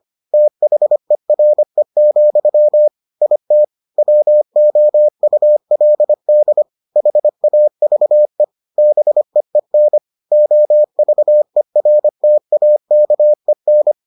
Morse code, heard as a keyed tone, about 25 wpm, 600 Hz. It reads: THERE, IT WOULD HAVE BEEN OVERTAKEN